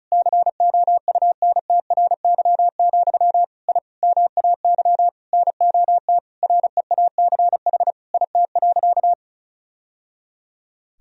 Morse code, heard as a keyed tone, about 36 wpm, 700 Hz. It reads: COUNTRY, I MAY NOT REACH IT.
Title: Sentences from Top 1000 Words - Review Part 2 - 35wpm